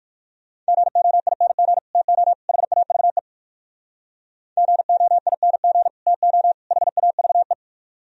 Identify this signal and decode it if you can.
Morse code, heard as a keyed tone, about 63 wpm, 700 Hz. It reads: GOING TO HAVE GOING TO HAVE